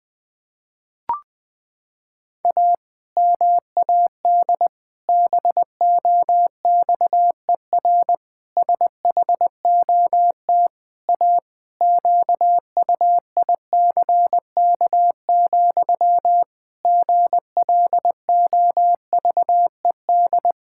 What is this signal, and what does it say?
Morse code, heard as a keyed tone, about 20 wpm, 700 Hz. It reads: A MAD BOXER SHOT A QUICK, GLOVED
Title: Pangram Challenge v2 - 20wpm